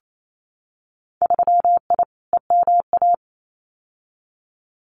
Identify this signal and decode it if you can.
Morse code, heard as a keyed tone, about 28 wpm, 700 Hz. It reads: 3I EMA
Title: Winter Field Day Exchanges Encore 28wpm